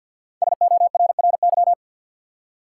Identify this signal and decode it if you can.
Morse code, heard as a keyed tone, about 47 wpm, 700 Hz. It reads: SORRY
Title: Five Letter Words - 50wpm